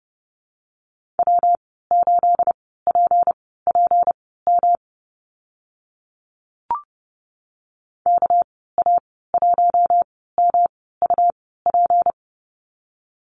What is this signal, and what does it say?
Morse code, heard as a keyed tone, about 30 wpm, spaced as for 17 wpm, 700 Hz. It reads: W8PPM KA1MUP
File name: Lesson 37g - Letter x - Call Signs 30wpm